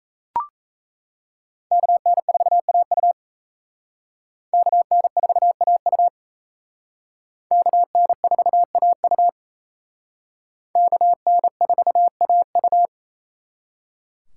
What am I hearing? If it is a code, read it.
Morse code, heard as a keyed tone, about 35 wpm, 700 Hz. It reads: KN4AU KN4AU KN4AU KN4AU
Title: Speed Racing - USA 2x2 Call Signs 28wpm